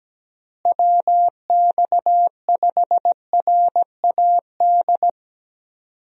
Morse code, heard as a keyed tone, about 17 wpm, 700 Hz. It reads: WX5RAD
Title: USA Call Signs - 6 Characters - Repeated 3x - 17wpm